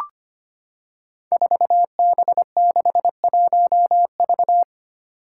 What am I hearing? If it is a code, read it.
Morse code, heard as a keyed tone, about 25 wpm, 700 Hz. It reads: 4B61V
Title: Five Challenging Letter-Number Combinations 25wpm